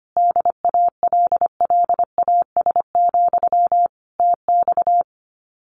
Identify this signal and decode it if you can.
Morse code, heard as a keyed tone, about 25 wpm, 700 Hz. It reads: DALLAS, TX
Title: Common QSO Phrases Encore 25wpm